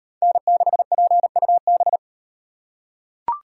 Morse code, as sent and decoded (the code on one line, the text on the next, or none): -. -.... .--. ..- -...
N6PUB